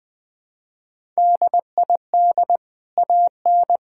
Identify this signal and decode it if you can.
Morse code, heard as a keyed tone, about 20 wpm, 700 Hz. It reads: DID AN
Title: Sentences from Top 1000 Words - Review Part 6 - 20wpm